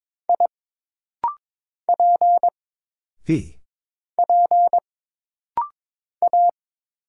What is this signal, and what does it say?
Morse code, heard as a keyed tone, about 22 wpm, 700 Hz. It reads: I P P A
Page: This rapid-fire timing is geared to push your ICR (Instant Character Recognition) skills.